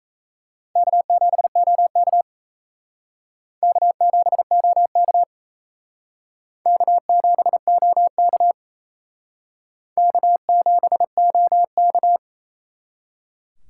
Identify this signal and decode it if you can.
Morse code, heard as a keyed tone, about 38 wpm, 700 Hz. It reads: K7OK K7OK K7OK K7OK